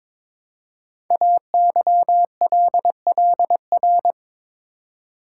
Morse code, heard as a keyed tone, about 22 wpm, 700 Hz. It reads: AYLLR
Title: Code Groups - 5 Letters - 22wpm